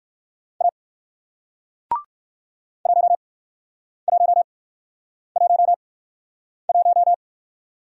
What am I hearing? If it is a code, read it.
Morse code, heard as a keyed tone, about 57 wpm, spaced as for 9 wpm, 700 Hz. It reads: A 1111